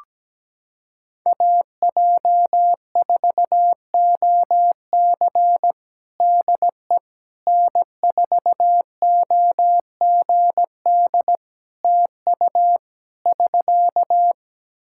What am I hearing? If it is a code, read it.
Morse code, heard as a keyed tone, about 17 wpm, 700 Hz. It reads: AJ4OC DE N4OGD TU <SK>
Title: Practice QSO - Exchange 4 - Ending - 17wpm